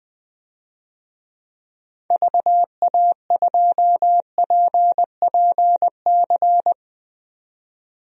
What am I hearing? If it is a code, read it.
Morse code, heard as a keyed tone, about 20 wpm, 700 Hz. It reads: VA2PPC
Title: DXCC-100 Call Signs - 6 Characters - 20wpm